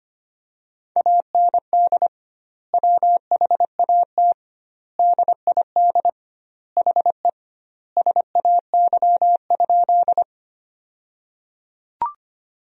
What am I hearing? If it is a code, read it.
Morse code, heard as a keyed tone, about 25 wpm, 700 Hz. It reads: AND WHAT DID HE SAY?
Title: 2x Word-Spacing - Sentences from Top 200 Words - 25wpm